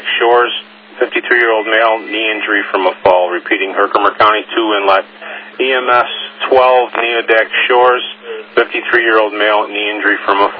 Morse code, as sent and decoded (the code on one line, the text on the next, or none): none